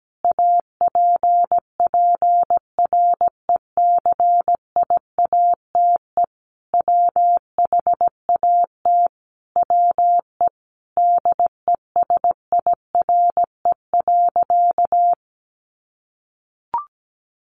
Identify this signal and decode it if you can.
Morse code, heard as a keyed tone, about 17 wpm, 700 Hz. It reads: APPRECIATE WHAT WE DESIRE.
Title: Sentences from Top 1300 Words 17wpm